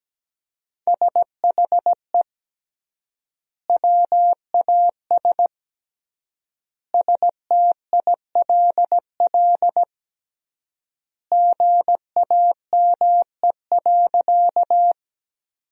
Morse code, heard as a keyed tone, about 17 wpm, 700 Hz. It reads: SHE WAS STILL GAME.
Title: 3x Word-Spacing - Sentences from Top 500 Words - 17wpm